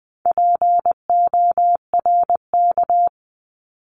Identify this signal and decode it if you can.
Morse code, heard as a keyed tone, about 20 wpm, 700 Hz. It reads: PORK
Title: 260 Lesson 42e - Words 20-20wpm